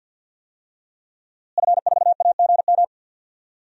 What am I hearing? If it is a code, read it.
Morse code, heard as a keyed tone, about 51 wpm, 700 Hz. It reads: F4AGM
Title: DXCC-100 Call Signs - 5 Characters - Repeated 3x - 50wpm